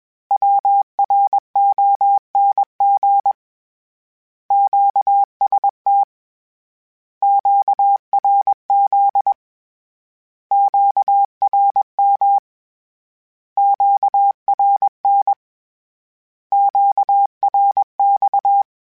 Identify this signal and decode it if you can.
Morse code, heard as a keyed tone, about 21 wpm, 800 Hz. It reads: WRONG QST QRZ QRM QRN QRX
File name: Misc / Misc-mp3s / Words @20